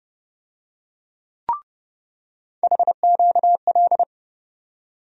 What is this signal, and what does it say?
Morse code, heard as a keyed tone, about 31 wpm, 700 Hz. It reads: HQL
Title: USA Call Sign Postfixes - 3 Letters 30wpm